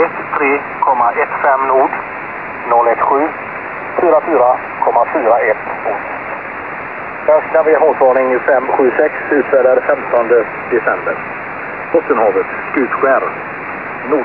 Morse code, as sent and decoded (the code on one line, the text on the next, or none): none